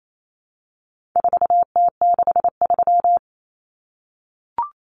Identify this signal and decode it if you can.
Morse code, heard as a keyed tone, about 28 wpm, 700 Hz. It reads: 4T63
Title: Common Cut Numbers - 4-Digit 28wpm